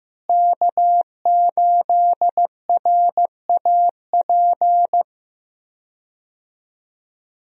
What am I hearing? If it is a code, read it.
Morse code, heard as a keyed tone, about 15 wpm, 700 Hz. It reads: K8RAP
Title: USA Call Signs - 5 Characters - Repeated 3x - 15wpm